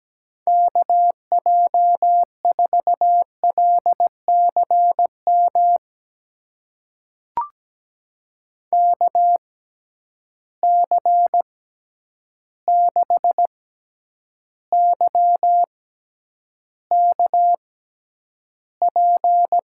Morse code, heard as a keyed tone, about 17 wpm, spaced as for 9 wpm, 700 Hz. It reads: KJ4LCM KC6YKP
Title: USA Call Signs - 6x Character Spacing - 17wpm